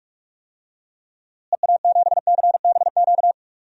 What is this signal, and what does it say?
Morse code, heard as a keyed tone, about 45 wpm, 700 Hz. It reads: EA7CBQ
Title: DXCC-100 Call Signs - 6 Characters - Repeated 3x - 45wpm